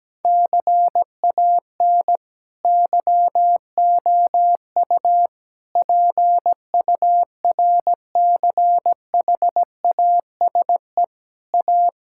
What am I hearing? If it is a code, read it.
Morse code, heard as a keyed tone, about 17 wpm, 700 Hz. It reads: CAN YOU PURCHASE A
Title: Sentences from Top 1000 Words - Review Part 5 - 17wpm